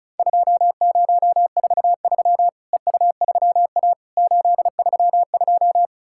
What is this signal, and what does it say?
Morse code, heard as a keyed tone, about 36 wpm, 700 Hz. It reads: 2043 EU3A 832
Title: Worked All Europe DX Contest - QTC Exchange - 35wpm